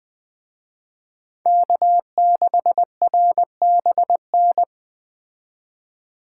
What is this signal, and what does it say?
Morse code, heard as a keyed tone, about 20 wpm, 700 Hz. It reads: K6RBN